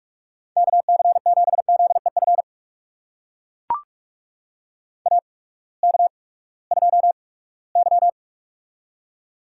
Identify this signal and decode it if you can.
Morse code, heard as a keyed tone, about 44 wpm, spaced as for 22 wpm, 700 Hz. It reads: KC7ZEF AK2Y
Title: USA Call Signs - 8x Character Spacing - 45wpm